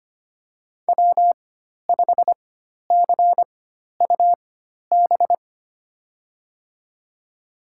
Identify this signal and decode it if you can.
Morse code, heard as a keyed tone, about 25 wpm, spaced as for 12 wpm, 700 Hz. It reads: W5CUB
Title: USA Call Signs - 4x Character Spacing - 25wpm